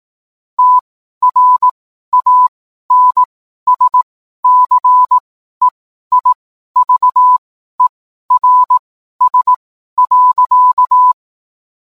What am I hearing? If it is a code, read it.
Morse code, heard as a keyed tone, about 18 wpm, spaced as for 12 wpm, 1000 Hz. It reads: TRANSCEIVERS.